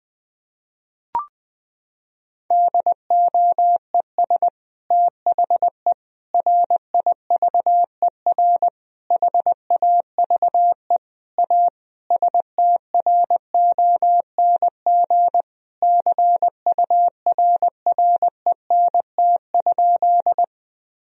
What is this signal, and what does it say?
Morse code, heard as a keyed tone, about 20 wpm, 700 Hz. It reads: DOES THE RIVER HAVE A STRONG CURRENT?
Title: Sentences from Top 1000 Words - Review Part 4 - 20wpm